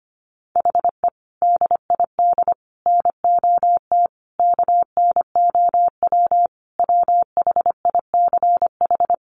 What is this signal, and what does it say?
Morse code, heard as a keyed tone, about 25 wpm, 700 Hz. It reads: HE DID NOT KNOW WHICH
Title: Sentences from Top 1300 Words 25wpm